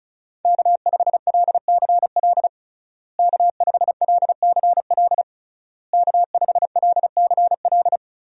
Morse code, heard as a keyed tone, about 36 wpm, 700 Hz. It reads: K5LCL K5LCL K5LCL
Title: USA Call Signs - 5 Characters - Repeated 3x - 35wpm